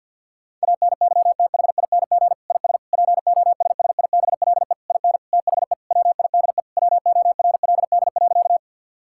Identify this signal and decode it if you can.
Morse code, heard as a keyed tone, about 52 wpm, 700 Hz. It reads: ANYTHING IS POSSIBLE IN THE WIDE WORLD.
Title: Sentences from Top 1000 Words - Review Part 6 - 50wpm